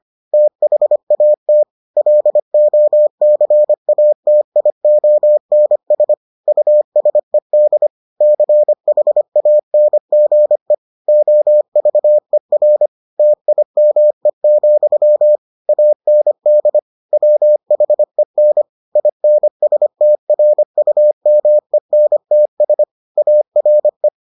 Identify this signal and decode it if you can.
Morse code, heard as a keyed tone, about 25 wpm, 600 Hz. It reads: THAT LOCATIONS USED CHANGE OVER TIME, AND WHEN INSTRUMENTS ARE